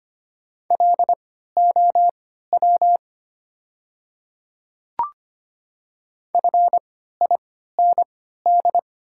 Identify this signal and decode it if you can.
Morse code, heard as a keyed tone, about 25 wpm, spaced as for 14 wpm, 700 Hz. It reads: LOW FIND
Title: 3x Character Spacing - Top 200 Words - 25wpm